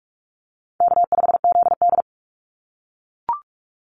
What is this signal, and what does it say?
Morse code, heard as a keyed tone, about 44 wpm, 700 Hz. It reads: K5ZD
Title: Easy Call Signs - 45wpm